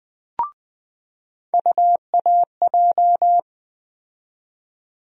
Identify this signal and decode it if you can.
Morse code, heard as a keyed tone, about 20 wpm, 700 Hz. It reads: UAJ
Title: USA Call Sign Postfixes - 3 Letters 20wpm